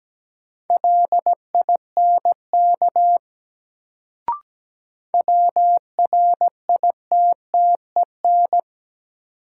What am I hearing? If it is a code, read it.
Morse code, heard as a keyed tone, about 17 wpm, 700 Hz. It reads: LINK WRITTEN
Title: Sentences from Top 1500 Words - New Word Drill Alt - 17wpm